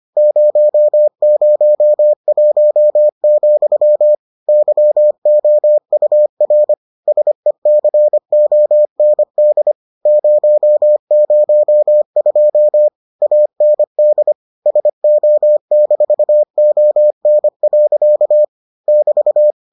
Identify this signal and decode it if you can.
Morse code, heard as a keyed tone, about 25 wpm, 600 Hz. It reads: 001, YOUR SECOND 002 AND SO-ON. <BT>